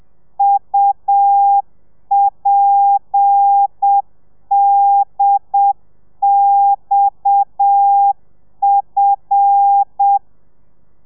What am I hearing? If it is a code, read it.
Morse code, heard as a keyed tone, about 7 wpm, 800 Hz. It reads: UPDXF
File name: telegrafia / i2tfj / Lez7 30